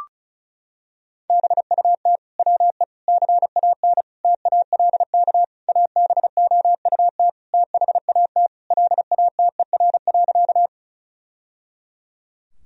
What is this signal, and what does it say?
Morse code, heard as a keyed tone, about 34 wpm, 700 Hz. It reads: BUT WE CAN TALK ABOUT THAT LATER.
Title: Sentences from Top 1000 Words - Review Part 4 - 35wpm